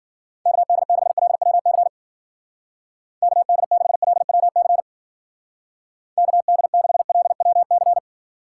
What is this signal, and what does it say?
Morse code, heard as a keyed tone, about 49 wpm, 700 Hz. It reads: KD6LWC KD6LWC KD6LWC
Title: Speed Racing - USA 2x3 Call Signs 40wpm